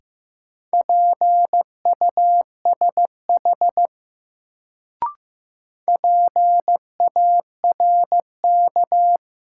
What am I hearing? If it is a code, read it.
Morse code, heard as a keyed tone, about 15 wpm, 700 Hz. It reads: PUSH PARK
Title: Sentences from Top 1300 Words - New Word Drill Alt - 15wpm